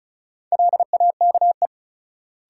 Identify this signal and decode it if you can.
Morse code, heard as a keyed tone, about 34 wpm, 700 Hz. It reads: LAKE
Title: Sentences from Top 1100 Words - New Word Drill Alt - 35wpm